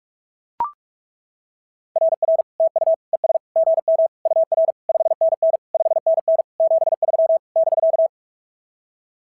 Morse code, heard as a keyed tone, about 42 wpm, 660 Hz. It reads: RR TU ES GM UR 5NN 5NN 73 <BK>
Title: SOTA QSO Practice 45wpm